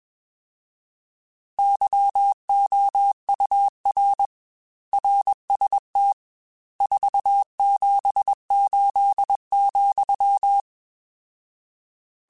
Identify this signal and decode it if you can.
Morse code, heard as a keyed tone, about 21 wpm, 800 Hz. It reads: YOUR RST 478,